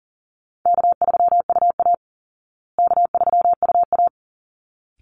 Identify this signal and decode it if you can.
Morse code, heard as a keyed tone, about 40 wpm, 700 Hz. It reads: K3UA K3UA
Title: Easy Call Signs Repeated 2x - 40wpm